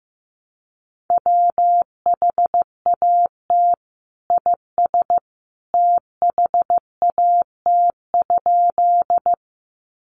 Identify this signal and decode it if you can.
Morse code, heard as a keyed tone, about 15 wpm, 700 Hz. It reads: WHAT IS THAT?
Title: Sentences from Top 100 Words 15wpm